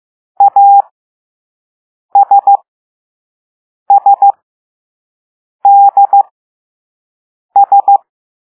Morse code, heard as a keyed tone, about 15 wpm, spaced as for 5 wpm, 800 Hz. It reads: ASSDS
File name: Morse / Level3B